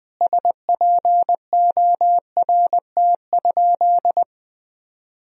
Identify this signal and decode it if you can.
Morse code, heard as a keyed tone, about 20 wpm, 700 Hz. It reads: SPORT?
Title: Sentences from Top 1000 Words - Review Part 4 - 20wpm